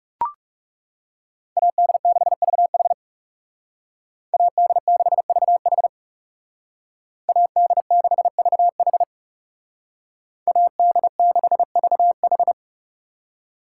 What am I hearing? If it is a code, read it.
Morse code, heard as a keyed tone, about 39 wpm, 700 Hz. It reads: AD6VH AD6VH AD6VH AD6VH